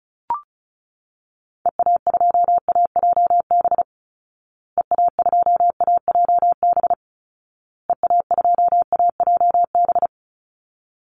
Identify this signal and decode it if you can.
Morse code, heard as a keyed tone, about 34 wpm, 700 Hz. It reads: EA2AJB EA2AJB EA2AJB
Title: Call Signs Repeated 2x - 35wpm